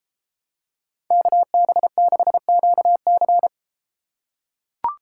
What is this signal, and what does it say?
Morse code, heard as a keyed tone, about 33 wpm, 700 Hz. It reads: KB6QC